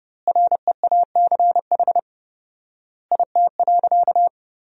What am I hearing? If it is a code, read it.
Morse code, heard as a keyed tone, about 30 wpm, 700 Hz. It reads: REACH IT.
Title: 4x Word-Spacing - Sentences from Top 400 Words - 30wpm